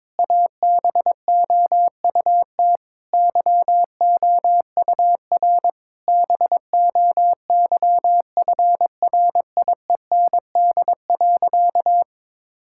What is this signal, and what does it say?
Morse code, heard as a keyed tone, about 22 wpm, 700 Hz. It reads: ABOUT YOUR BOYFRIEND.
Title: Sentences from Top 1000 Words - Review Part 5 - 22wpm